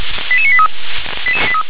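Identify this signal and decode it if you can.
Five-tone selective call (ZVEI), heard as quick runs of five tones, tones of about 70 ms. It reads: 88983
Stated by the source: ZVEI five-tone selective call received on an SDR.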